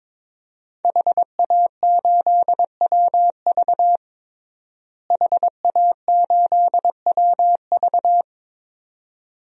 Morse code, heard as a keyed tone, about 22 wpm, 700 Hz. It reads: HA8WV HA8WV
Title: DXCC-100 Call Signs - 5 Characters - Repeated 2x - 22wpm